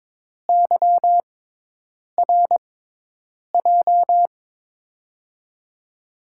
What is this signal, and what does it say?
Morse code, heard as a keyed tone, about 23 wpm, spaced as for 8 wpm, 700 Hz. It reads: YRJ
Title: USA Call Signs - 5 Characters - 6x Character Spacing - 22wpm